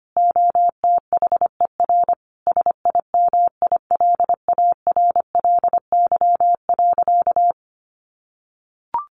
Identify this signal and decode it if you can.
Morse code, heard as a keyed tone, about 25 wpm, 700 Hz. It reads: OTHER SIMILARLY.
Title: Sentences from Top 1400 Words 25wpm